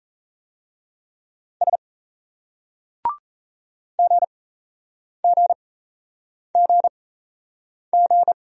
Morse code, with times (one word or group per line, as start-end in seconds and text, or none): 1.6-1.8 S
4.0-8.3 GGGG